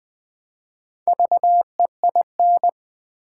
Morse code, heard as a keyed tone, about 20 wpm, 700 Hz. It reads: VEIN
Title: Build A Word - 1 Letter At A Time - 3 to 5-Letter Words - 20wpm